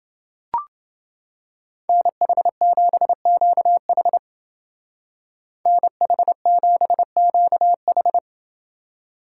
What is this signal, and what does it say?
Morse code, heard as a keyed tone, about 28 wpm, 700 Hz. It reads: NH7QH NH7QH